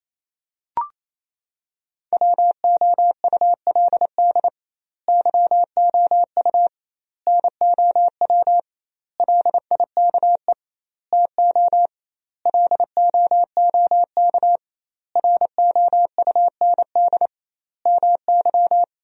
Morse code, heard as a keyed tone, about 28 wpm, 700 Hz. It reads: WOULD YOU NOW LIKE TO LOOK ROUND MY
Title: 2x Word-Spacing - Sentences from Top 200 Words - 28wpm